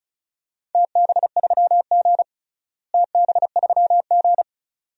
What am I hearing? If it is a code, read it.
Morse code, heard as a keyed tone, about 34 wpm, 700 Hz. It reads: TB3G TB3G
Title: DXCC-100 Call Signs - 4 Characters - Repeated 2x - 35wpm